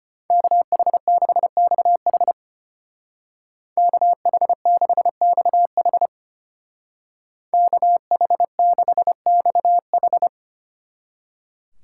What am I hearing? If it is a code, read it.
Morse code, heard as a keyed tone, about 32 wpm, 700 Hz. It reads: KH6XH KH6XH KH6XH